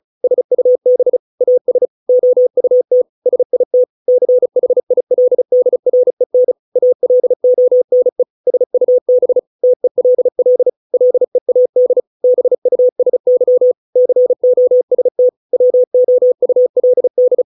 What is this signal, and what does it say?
Morse code, heard as a keyed tone, about 34 wpm, 500 Hz. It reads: SUB AS OUT SIT CHILDREN ALONE SUB TELL LEAD BUSY COST WOULD